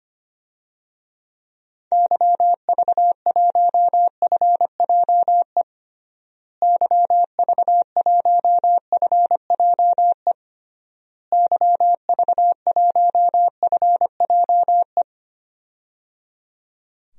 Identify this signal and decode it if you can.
Morse code, heard as a keyed tone, about 25 wpm, 700 Hz. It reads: YV1FJE YV1FJE YV1FJE
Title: DXCC-100 Call Signs - 6 Characters - Repeated 3x - 25wpm